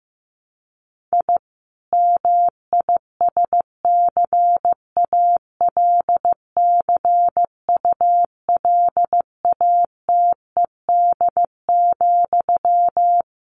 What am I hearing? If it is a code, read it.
Morse code, heard as a keyed tone, about 15 wpm, 700 Hz. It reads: I MISCALCULATED,
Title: A Sound Of Thunder 15wpm